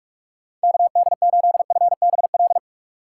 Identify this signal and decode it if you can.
Morse code, heard as a keyed tone, about 46 wpm, 700 Hz. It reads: KD8FBL
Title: USA Call Signs - 3x Character Spacing - 45wpm